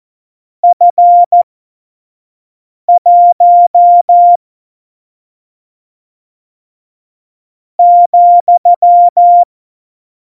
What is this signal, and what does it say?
Morse code, heard as a keyed tone, about 14 wpm, spaced as for 5 wpm, 700 Hz. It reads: F1 ,
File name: Morsekurs / Witze / Witz 29. Januar 2019, 25 F70